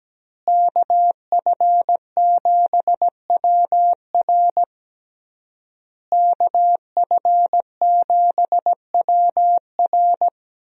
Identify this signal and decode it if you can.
Morse code, heard as a keyed tone, about 17 wpm, 700 Hz. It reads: KF7WR KF7WR
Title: USA Call Signs - 5 Characters - Repeated 3x - 17wpm